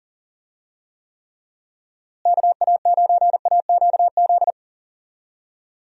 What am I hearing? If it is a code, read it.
Morse code, heard as a keyed tone, about 40 wpm, 700 Hz. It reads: KA9AQZ
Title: USA Call Signs - 6 Characters - Repeated 3x - 40wpm